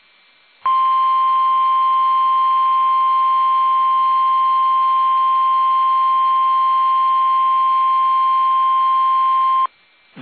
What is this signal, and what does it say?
Emergency Alert System (SAME) tones (the 1050 Hz attention tone). It is heard from 0.6 to 9.7 s.
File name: eas / KUSO / KUSO